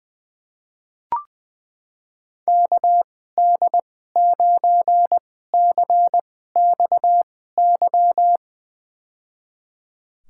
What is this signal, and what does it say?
Morse code, heard as a keyed tone, about 21 wpm, spaced as for 15 wpm, 700 Hz. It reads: KD9CXY
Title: USA Call Signs - 6 Characters - 2x Character Spacing - 20wpm